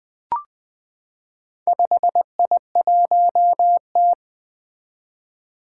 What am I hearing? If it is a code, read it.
Morse code, heard as a keyed tone, about 20 wpm, 700 Hz. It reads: 5I1T
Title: Four Challenging Letter-Number Combinations 20wpm